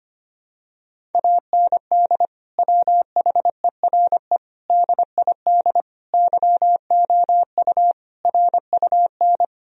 Morse code, heard as a keyed tone, about 25 wpm, 700 Hz. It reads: AND WHERE DID YOU RUN
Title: Sentences from Top 1000 Words - Review Part 1 - 25wpm